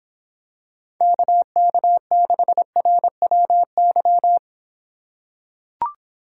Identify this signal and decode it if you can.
Morse code, heard as a keyed tone, about 26 wpm, 700 Hz. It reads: KK6RWY